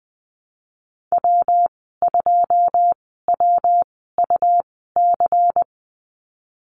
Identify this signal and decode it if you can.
Morse code, heard as a keyed tone, about 20 wpm, spaced as for 14 wpm, 700 Hz. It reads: W2WUC
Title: 128 Lesson 21g - Number 6 - Call Signs 20wpm